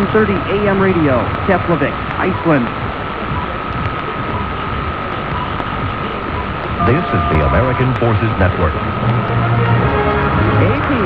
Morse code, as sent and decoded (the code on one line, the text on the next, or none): none